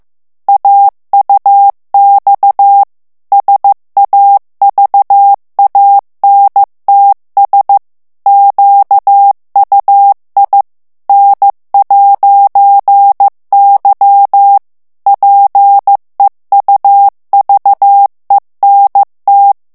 Morse code, heard as a keyed tone, about 15 wpm, 800 Hz. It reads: AUX SAVANTS QUI N'Y PEUVENT
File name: morse mp3 / FRANCAIS TEXTE CLAIR 2 VITESSE 14 MOTS MINUTES